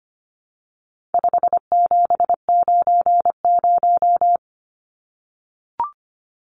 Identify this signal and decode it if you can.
Morse code, heard as a keyed tone, about 25 wpm, 700 Hz. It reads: 5790
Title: Numbers - 4-Digit 25wpm